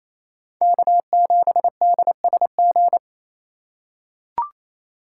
Morse code, heard as a keyed tone, about 27 wpm, 700 Hz. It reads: K7DSG
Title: USA Call Signs - 2x Character Spacing - 28wpm